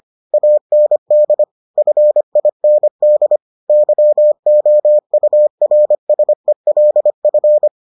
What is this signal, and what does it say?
Morse code, heard as a keyed tone, about 25 wpm, 600 Hz. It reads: AND FIND YOURSELF